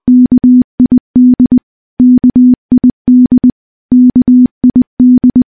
Morse code, heard as a keyed tone, about 21 wpm, 250 Hz. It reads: KID KID KID
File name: morse-training-20WPM